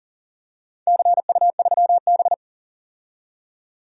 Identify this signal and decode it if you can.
Morse code, heard as a keyed tone, about 40 wpm, 700 Hz. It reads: CU3B
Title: DXCC-100 Call Signs - 4 Characters - 40wpm